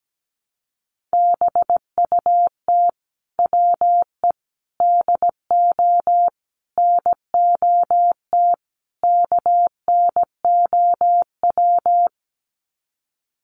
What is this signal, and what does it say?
Morse code, heard as a keyed tone, about 17 wpm, 700 Hz. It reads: BUT WE DO NOT KNOW
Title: Sets of 5 Words Encore2 17wpm